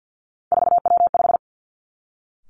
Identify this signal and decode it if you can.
Morse code, heard as a keyed tone, about 50 wpm, 700 Hz. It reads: 4W5
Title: Three Challenging Letter-Number Combinations 50wpm